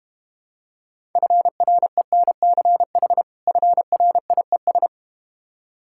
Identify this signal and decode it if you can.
Morse code, heard as a keyed tone, about 32 wpm, 690 Hz. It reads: FRENCH FRIES